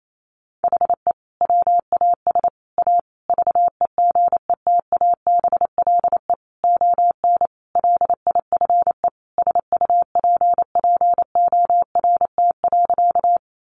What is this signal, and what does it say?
Morse code, heard as a keyed tone, about 28 wpm, 700 Hz. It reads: HE WAS A VEGETABLE ON LIFE SUPPORT.
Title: Sentences from Top 1300 Words 28wpm